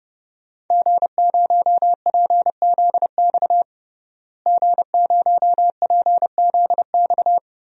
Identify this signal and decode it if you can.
Morse code, heard as a keyed tone, about 30 wpm, 700 Hz. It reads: G0PZX G0PZX
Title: DXCC-100 Call Signs - 5 Characters - Repeated 3x - 30wpm